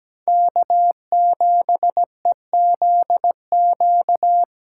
Morse code, heard as a keyed tone, about 20 wpm, 700 Hz. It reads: K7EZQ
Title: Speed Racing - USA 1x3 Call Signs 17wpm